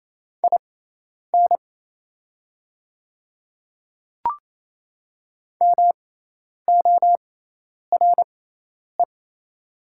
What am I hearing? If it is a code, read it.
Morse code, heard as a keyed tone, about 28 wpm, spaced as for 10 wpm, 700 Hz. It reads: IN MORE
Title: 6x Character Spacing - Top 200 Words - 28wpm